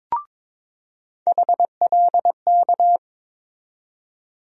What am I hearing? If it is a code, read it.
Morse code, heard as a keyed tone, about 22 wpm, 700 Hz. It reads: HLK
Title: USA Call Sign Postfixes - 3 Letters 22wpm